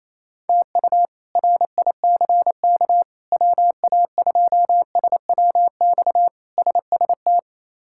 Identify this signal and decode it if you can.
Morse code, heard as a keyed tone, about 28 wpm, 700 Hz. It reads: TU RICK WA2SWX SST
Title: K1USN SST QSO Practice - 28wpm